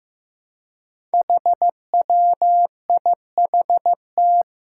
Morse code, heard as a keyed tone, about 15 wpm, 700 Hz. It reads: HWIHT
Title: Five Challenging Letter-Number Combinations 15wpm